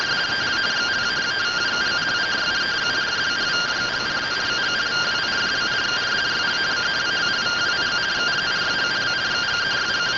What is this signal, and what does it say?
Two-tone teleprinter signal, about 170 Hz shift.